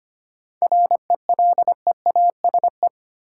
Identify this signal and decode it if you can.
Morse code, heard as a keyed tone, about 25 wpm, 700 Hz. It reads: RELEASE
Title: Sentences from Top 1100 Words - New Word Drill - 25wpm